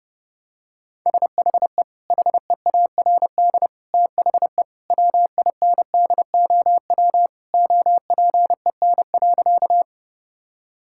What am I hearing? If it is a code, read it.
Morse code, heard as a keyed tone, about 32 wpm, 700 Hz. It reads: SHE HEARD THE WINDOW OPEN.
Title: Sentences from Top 1600 Words 30wpm